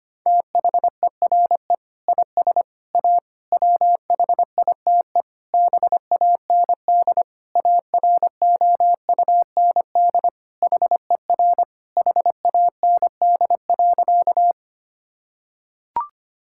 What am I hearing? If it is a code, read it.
Morse code, heard as a keyed tone, about 25 wpm, 700 Hz. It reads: THERE IS A WHITE BAND AROUND HER HAND.
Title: Sentences from Top 1600 Words 25wpm